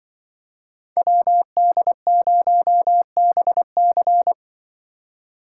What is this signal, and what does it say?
Morse code, heard as a keyed tone, about 23 wpm, 700 Hz. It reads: WD0BC
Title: Speed Racing - USA 2x2 Call Signs 20wpm